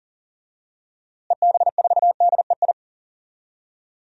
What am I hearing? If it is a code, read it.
Morse code, heard as a keyed tone, about 38 wpm, 700 Hz. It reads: EB4DEI
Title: DXCC-100 Call Signs - 6 Characters - Repeated 3x - 40wpm